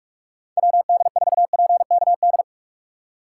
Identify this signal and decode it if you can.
Morse code, heard as a keyed tone, about 49 wpm, 700 Hz. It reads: WD4PKD